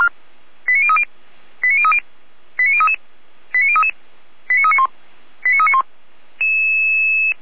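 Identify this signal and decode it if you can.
Five-tone selective call (ZVEI), heard as quick runs of five tones, tones of about 70 ms. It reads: 89030 89030 89033 89033 89391 89391
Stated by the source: ZVEI five-tone selective call received on an SDR.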